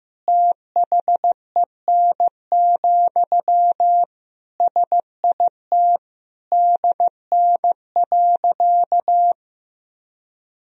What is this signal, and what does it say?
Morse code, heard as a keyed tone, about 14 wpm, 700 Hz. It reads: THEN, SIT DN.
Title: Sentences from Top 500 Words and Abbreviations 15wpm